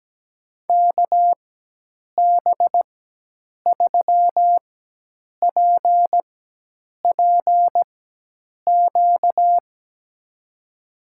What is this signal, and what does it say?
Morse code, heard as a keyed tone, about 17 wpm, spaced as for 8 wpm, 700 Hz. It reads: KB3PPQ
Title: USA Call Signs - 6 Characters - 4x Character Spacing - 17wpm